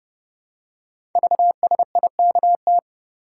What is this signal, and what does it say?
Morse code, heard as a keyed tone, about 29 wpm, 700 Hz. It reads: VSIKT
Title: Code Groups - 5 Letters - 30wpm